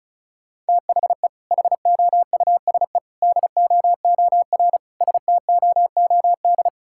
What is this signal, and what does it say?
Morse code, heard as a keyed tone, about 36 wpm, 700 Hz. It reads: THE HOUSE DOOR STOOD
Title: Sentences from Top 1000 Words - Review Part 6 - 35wpm